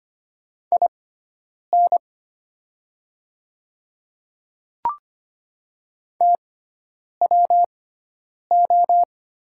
Morse code, heard as a keyed tone, about 25 wpm, spaced as for 9 wpm, 700 Hz. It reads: IN TWO
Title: 6x Character Spacing - Top 100 Words - 25wpm